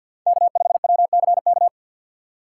Morse code, heard as a keyed tone, about 51 wpm, 700 Hz. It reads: K 5 W C K
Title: USA Call Signs - 6x Character Spacing - 50wpm